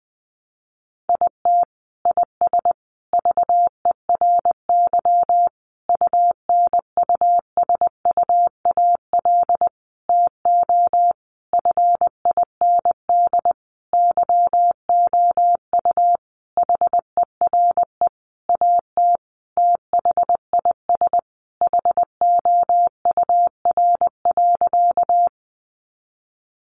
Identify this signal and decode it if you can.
Morse code, heard as a keyed tone, about 20 wpm, 700 Hz. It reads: IT IS VERY UNUSUAL TO FIND YOU HERE AT THIS HOUR.
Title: Sentences from Top 1400 Words 20wpm